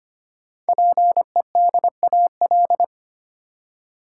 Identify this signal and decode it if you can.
Morse code, heard as a keyed tone, about 25 wpm, 700 Hz. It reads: PEDAL
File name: practice / 25wpm / Easy Words with Hints - 25wpm